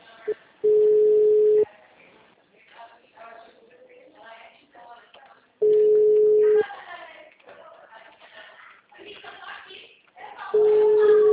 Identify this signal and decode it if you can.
Ringback tone (Europe, 425 Hz), with speech.
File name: out- -2010- - - .0